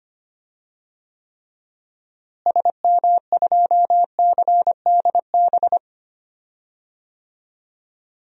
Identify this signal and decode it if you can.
Morse code, heard as a keyed tone, about 25 wpm, 700 Hz. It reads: SM2CDB
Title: DXCC-100 Call Signs - 6 Characters - Repeated 2x - 25wpm